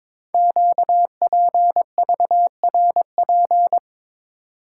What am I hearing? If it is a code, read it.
Morse code, heard as a keyed tone, about 22 wpm, 700 Hz. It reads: QPVRP
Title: Code Groups - 5 Letters - 22wpm